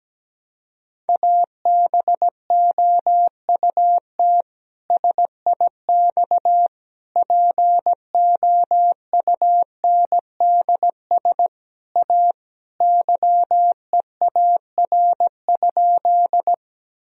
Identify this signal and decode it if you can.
Morse code, heard as a keyed tone, about 17 wpm, 700 Hz. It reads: ABOUT SIX POUNDS A YEAR?
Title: Sentences from Top 1000 Words - Review Part 2 - 17wpm